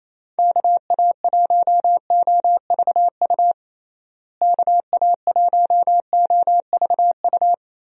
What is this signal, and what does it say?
Morse code, heard as a keyed tone, about 28 wpm, 700 Hz. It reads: KA1OVU KA1OVU
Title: USA Call Signs - 6 Characters - Repeated 3x - 28wpm